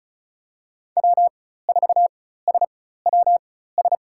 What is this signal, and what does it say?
Morse code, heard as a keyed tone, about 36 wpm, spaced as for 16 wpm, 700 Hz. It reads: W4SWS
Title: USA Call Signs - 4x Character Spacing - 35wpm